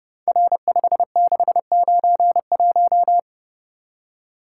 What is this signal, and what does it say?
Morse code, heard as a keyed tone, about 30 wpm, 700 Hz. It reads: R5691
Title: Code Groups - 5 Letter:Numbers - 30wpm